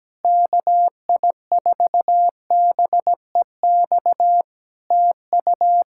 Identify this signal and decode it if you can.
Morse code, heard as a keyed tone, about 17 wpm, 700 Hz. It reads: KI4BEX TU
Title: Practice QSO - Exchange 4 - Ending - 17wpm